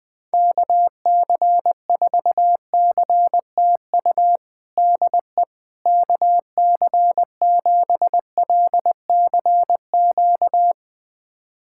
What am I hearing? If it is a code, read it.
Morse code, heard as a keyed tone, about 21 wpm, 700 Hz. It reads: KC4CTU DE KC7LCQ
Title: USA Callsign DE Callsign - 6 Characters - 20wpm